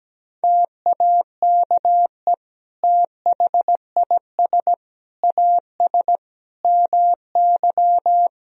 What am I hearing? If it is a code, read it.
Morse code, heard as a keyed tone, about 17 wpm, 700 Hz. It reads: TAKE THIS AS MY
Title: Sentences from Top 1000 Words - Review Part 2 - 17wpm